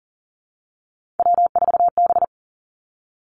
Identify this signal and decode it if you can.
Morse code, heard as a keyed tone, about 40 wpm, 700 Hz. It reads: W4B
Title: Three Challenging Letter-Number Combinations 40wpm